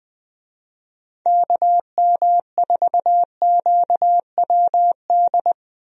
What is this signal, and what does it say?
Morse code, heard as a keyed tone, about 20 wpm, 700 Hz. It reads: KM4QWD
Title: USA Call Signs - 6 Characters - Repeated 3x - 20wpm